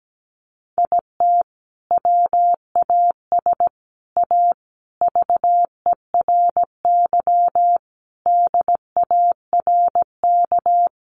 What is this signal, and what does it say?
Morse code, heard as a keyed tone, about 17 wpm, 700 Hz. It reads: IT WAS A VERY DARK
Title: Sentences from Top 500 Words 17wpm